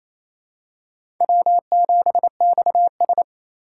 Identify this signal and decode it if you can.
Morse code, heard as a keyed tone, about 28 wpm, 700 Hz. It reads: W7XS
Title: USA Call Signs - 4 Characters - 2x Character Spacing - 28wpm